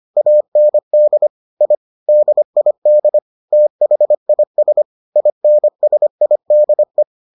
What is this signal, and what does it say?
Morse code, heard as a keyed tone, about 25 wpm, 600 Hz. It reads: AND I DID THIS INSIDE